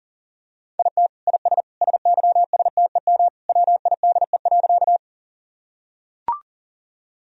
Morse code, heard as a keyed tone, about 40 wpm, 700 Hz. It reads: IT IS SYSTEM WIDE.
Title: Sentences from Top 1000 Words - Review Part 3 - 40wpm